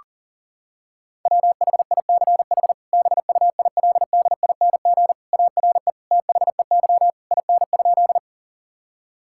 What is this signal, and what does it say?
Morse code, heard as a keyed tone, about 37 wpm, 700 Hz. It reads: WHICH BUILDING ARE THEY IN?
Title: Sentences from Top 1000 Words - Review Part 5 - 40wpm